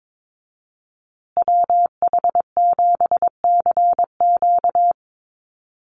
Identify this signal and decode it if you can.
Morse code, heard as a keyed tone, about 22 wpm, 700 Hz. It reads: WH7CQ
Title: USA Call Signs Repeated 2x 22wpm